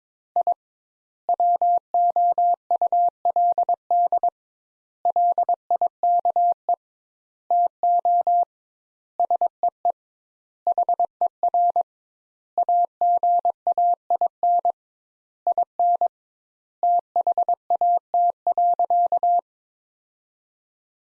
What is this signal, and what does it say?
Morse code, heard as a keyed tone, about 22 wpm, 700 Hz. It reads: I WOULD LIKE TO SEE HER AGAIN IN THAT.
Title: Sentences Top 200 Words - 22wpm